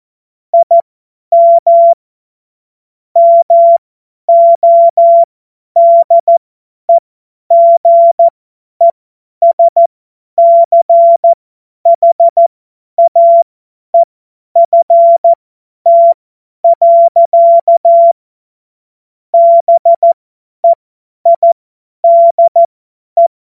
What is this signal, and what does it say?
Morse code, heard as a keyed tone, about 14 wpm, spaced as for 10 wpm, 680 Hz. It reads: IM MODEGESCHAEFT. BEIDE
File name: Morsekurs / Witze / Witz 16 Januar 2024 50 F70